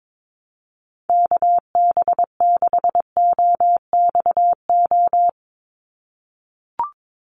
Morse code, heard as a keyed tone, about 22 wpm, 700 Hz. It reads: KB6OXO
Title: USA Call Signs 22wpm